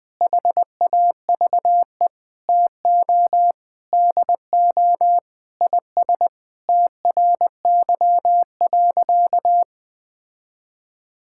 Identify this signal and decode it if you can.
Morse code, heard as a keyed tone, about 21 wpm, 700 Hz. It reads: HAVE TO DO IS TRY.
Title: Sentences with 5-Word Phrases - 20wpm